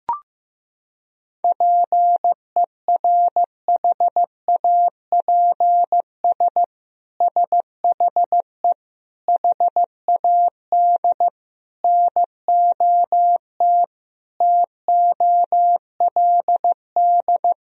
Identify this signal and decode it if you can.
Morse code, heard as a keyed tone, about 15 wpm, 700 Hz. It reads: PERHAPS SHE HAD NOT TOLD
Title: Sentences from Top 1000 Words - Review Part 1 - 15wpm